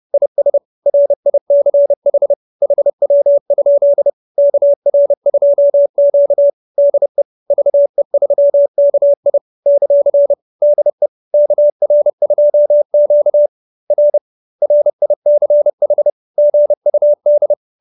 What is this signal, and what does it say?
Morse code, heard as a keyed tone, about 30 wpm, 570 Hz. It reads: IS RICH HW? KR2Q DE VE3KI ; DE KR2Q R RICH GUD